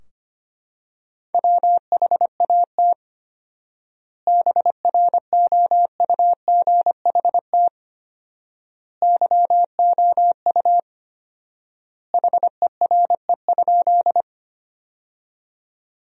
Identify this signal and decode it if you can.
Morse code, heard as a keyed tone, about 25 wpm, 700 Hz. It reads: WHAT BROUGHT YOU HERE?
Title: 4x Word-Spacing - Sentences from Top 500 Words - 25wpm